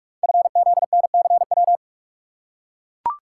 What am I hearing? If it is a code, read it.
Morse code, heard as a keyed tone, about 44 wpm, 690 Hz. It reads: FZNCW
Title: Code Groups - 5 Letters - 45wpm